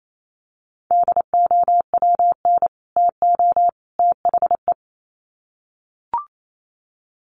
Morse code, heard as a keyed tone, about 28 wpm, 700 Hz. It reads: DOWN TO THE
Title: Sets of 3 Words 28wpm